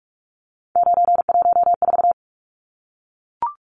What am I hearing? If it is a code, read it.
Morse code, heard as a keyed tone, about 44 wpm, 700 Hz. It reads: 914
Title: Numbers - 3-Digit 45wpm